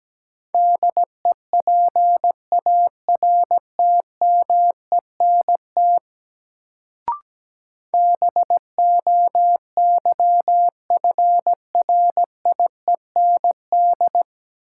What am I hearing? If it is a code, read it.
Morse code, heard as a keyed tone, about 17 wpm, 700 Hz. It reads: DEPARTMENT BOYFRIEND
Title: Sentences from Top 1000 Words - Review Part 5 - New Word Drill - 17wpm